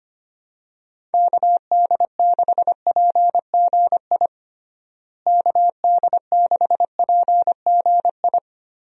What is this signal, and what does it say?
Morse code, heard as a keyed tone, about 25 wpm, 700 Hz. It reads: KD6PGI KD6PGI
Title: USA Call Signs - 6 Characters - Repeated 3x - 25wpm